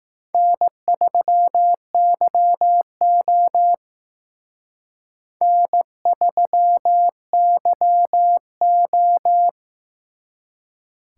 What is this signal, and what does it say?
Morse code, heard as a keyed tone, about 18 wpm, 700 Hz. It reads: N3YO N3YO
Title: Speed Racing - USA 1x2 Call Signs 15wpm